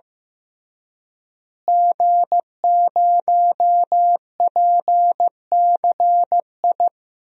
Morse code, heard as a keyed tone, about 15 wpm, 700 Hz. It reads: G0PCI
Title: DXCC-100 Call Signs - 5 Characters - Repeated 3x - 15wpm